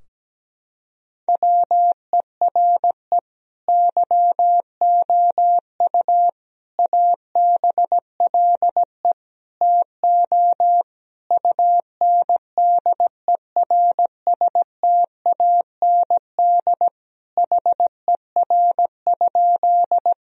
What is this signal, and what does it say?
Morse code, heard as a keyed tone, about 17 wpm, 700 Hz. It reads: WERE YOU ABLE TO UNDERSTAND HER?
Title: Sentences from Top 1000 Words - Review Part 2 - 17wpm